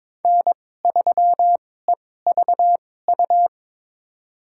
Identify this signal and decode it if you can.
Morse code, heard as a keyed tone, about 22 wpm, spaced as for 16 wpm, 700 Hz. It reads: N3EVU
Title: USA Call Signs - 5 Characters - 2x Character Spacing - 22wpm